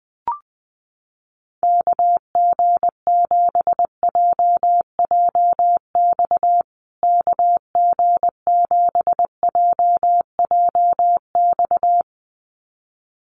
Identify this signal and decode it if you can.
Morse code, heard as a keyed tone, about 20 wpm, 700 Hz. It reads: KG7JJX KG7JJX
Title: USA Call Signs Repeated 2x 20wpm